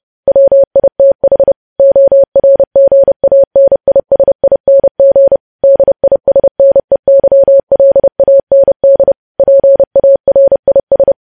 Morse code, heard as a keyed tone, about 30 wpm, 550 Hz. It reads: WITH ORGANISING DISNEYLAND PARIS
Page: News Headlines in Morse Code at 30 WPM